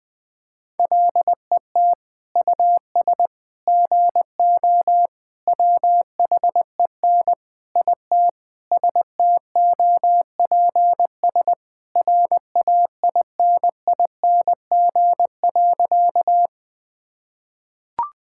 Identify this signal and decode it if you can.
Morse code, heard as a keyed tone, about 20 wpm, 700 Hz. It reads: LET US GO WHEN IT STOPS RAINING.
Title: Sentences with 2-Word Phrases - 20wpm